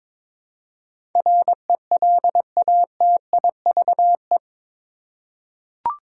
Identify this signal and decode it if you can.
Morse code, heard as a keyed tone, about 21 wpm, 700 Hz. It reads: RELATIVE
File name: practice / 22wpm / Eight Letter Words - 22wpm